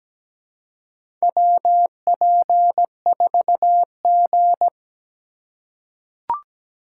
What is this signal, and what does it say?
Morse code, heard as a keyed tone, about 17 wpm, 700 Hz. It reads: WP4G
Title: USA Call Signs - 4 Characters - 4x Character Spacing - 17wpm